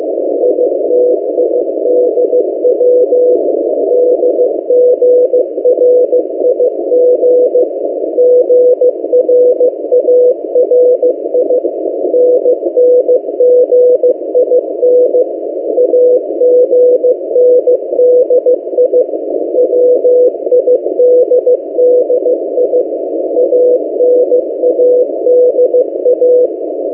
Morse code, heard as a keyed tone, about 16 wpm, 500 Hz. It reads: ITIDW DGRIG GRARI NNGIN AGNDI WIDDI ANADA
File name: M08a- -1355utc-8096khz